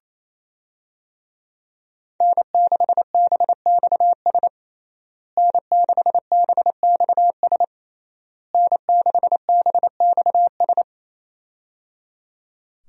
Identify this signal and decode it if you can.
Morse code, heard as a keyed tone, about 28 wpm, 700 Hz. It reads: N6BXS N6BXS N6BXS
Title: USA Call Signs - 5 Characters - Repeated 3x - 28wpm